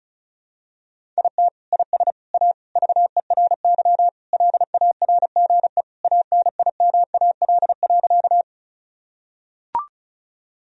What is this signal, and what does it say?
Morse code, heard as a keyed tone, about 34 wpm, 700 Hz. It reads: IT IS A VERY LARGE ANIMAL.
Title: Sentences from Top 1000 Words - Review Part 1 - 35wpm